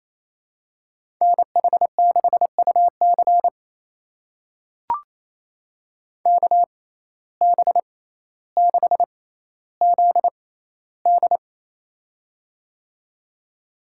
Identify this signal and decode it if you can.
Morse code, heard as a keyed tone, about 28 wpm, spaced as for 14 wpm, 700 Hz. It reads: NH6UC KB6ZD
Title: USA Call Signs - 5 Characters - 6x Character Spacing - 28wpm